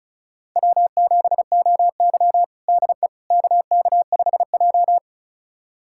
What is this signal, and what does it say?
Morse code, heard as a keyed tone, about 34 wpm, 700 Hz. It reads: W7OY DE KK5J
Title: USA Callsign DE Callsign - 4 Characters - 35wpm